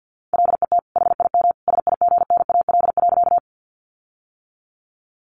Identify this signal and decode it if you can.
Morse code, heard as a keyed tone, about 50 wpm, 700 Hz. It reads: LET HIM SIGNAL.
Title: Sentences from Top 1400 Words 50wpm